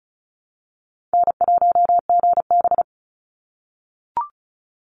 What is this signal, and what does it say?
Morse code, heard as a keyed tone, about 34 wpm, 700 Hz. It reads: N1GB
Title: NA SOTA Chaser Callsigns 35wpm